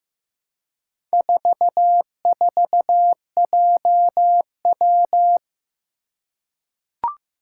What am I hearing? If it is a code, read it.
Morse code, heard as a keyed tone, about 15 wpm, 700 Hz. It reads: 44JW
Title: Four Challenging Letter-Number Combinations 15wpm